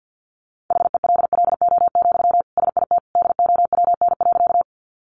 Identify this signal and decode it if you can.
Morse code, heard as a keyed tone, about 50 wpm, 700 Hz. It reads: HELLO, SIT DOWN.
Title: Sentences from Top 1100 Words 50wpm